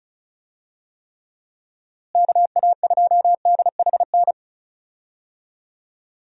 Morse code, heard as a keyed tone, about 34 wpm, 700 Hz. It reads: KA2DHN
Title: USA Call Signs - 6 Characters - Repeated 2x - 35wpm